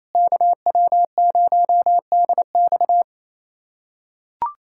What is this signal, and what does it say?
Morse code, heard as a keyed tone, about 28 wpm, 700 Hz. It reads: KW0DX
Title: USA Callsign DE Callsign - 5 Characters - 28wpm